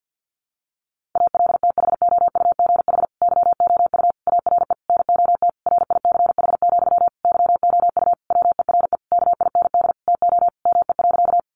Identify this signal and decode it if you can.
Morse code, heard as a keyed tone, about 50 wpm, 700 Hz. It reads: ALTHOUGH YOU ARE NOT RICH, YOU WERE KIND TO ME.
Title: Sentences from Top 1100 Words 50wpm